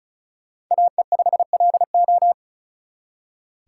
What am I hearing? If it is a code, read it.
Morse code, heard as a keyed tone, about 44 wpm, 700 Hz. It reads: AE5LO